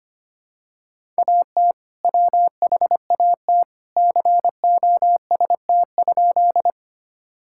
Morse code, heard as a keyed tone, about 25 wpm, 700 Hz. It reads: AT WHAT COST?
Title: Sentences from Top 1000 Words - Review Part 5 - 25wpm